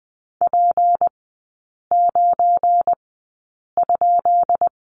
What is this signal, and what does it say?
Morse code, heard as a keyed tone, about 20 wpm, spaced as for 8 wpm, 700 Hz. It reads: P9?
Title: 178 Lesson 29e - Slash - Three Characters 20wpm